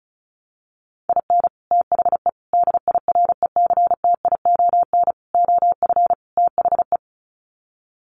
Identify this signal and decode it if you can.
Morse code, heard as a keyed tone, about 34 wpm, 700 Hz. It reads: IN THE DIRECTION OF THE
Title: Sets of 5 Words 35wpm